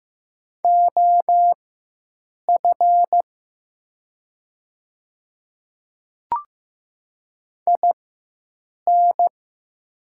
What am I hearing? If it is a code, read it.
Morse code, heard as a keyed tone, about 15 wpm, spaced as for 7 wpm, 700 Hz. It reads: OF IN
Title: 4x Character Spacing - Top 100 Words - 15wpm